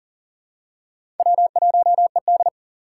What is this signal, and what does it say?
Morse code, heard as a keyed tone, about 40 wpm, 700 Hz. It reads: W1ED
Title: USA Call Signs - 4 Characters - 6x Character Spacing - 40wpm